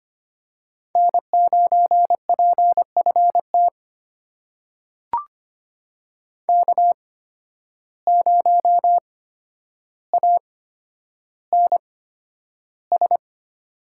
Morse code, heard as a keyed tone, about 25 wpm, spaced as for 11 wpm, 700 Hz. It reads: N9PFT K0ANS
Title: USA Call Signs - 5 Characters - 8x Character Spacing - 25wpm